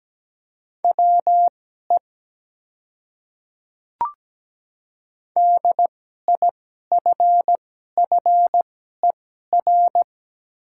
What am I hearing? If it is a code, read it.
Morse code, heard as a keyed tone, about 17 wpm, spaced as for 12 wpm, 700 Hz. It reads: WE DIFFER
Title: 2x Character Spacing - Top 300 Words - 17wpm